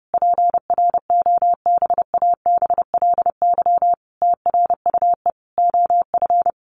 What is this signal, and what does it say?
Morse code, heard as a keyed tone, about 30 wpm, 700 Hz. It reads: PROBABLY TRUE OF
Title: Sentences from Top 700 Words 30wpm